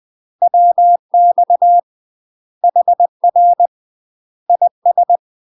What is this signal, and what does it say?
Morse code, heard as a keyed tone, about 20 wpm, 700 Hz. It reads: WX HR IS
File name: UK MASTER QSOs TRAINING / 20WPM / UK MASTER QSOs 0016